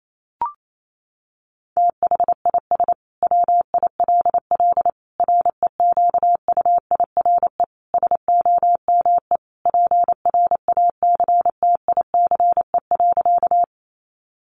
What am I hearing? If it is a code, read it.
Morse code, heard as a keyed tone, about 28 wpm, 700 Hz. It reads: THIS WILL REQUIRE SOME PRACTICE.
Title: Sentences from Top 900 Words 28wpm